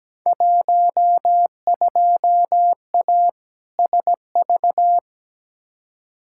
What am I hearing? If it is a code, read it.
Morse code, heard as a keyed tone, about 17 wpm, 700 Hz. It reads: 12A SV
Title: Field Day Exchanges Encore - 17wpm